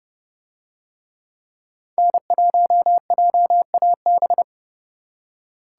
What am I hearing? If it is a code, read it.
Morse code, heard as a keyed tone, about 31 wpm, 700 Hz. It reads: N1JAB